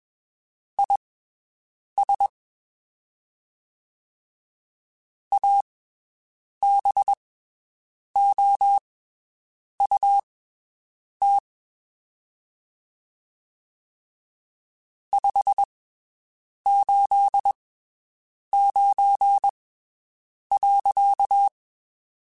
Morse code, heard as a keyed tone, about 21 wpm, spaced as for 7 wpm, 800 Hz. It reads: IS ABOUT 589.
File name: (spaced) / QSOs Spaced-mp3s / WD0FBT 1 @20 sp